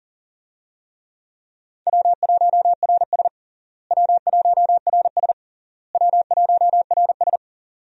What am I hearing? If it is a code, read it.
Morse code, heard as a keyed tone, about 37 wpm, 700 Hz. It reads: W1RS W1RS W1RS
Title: USA Call Signs - 4 Characters - Repeated 3x - 40wpm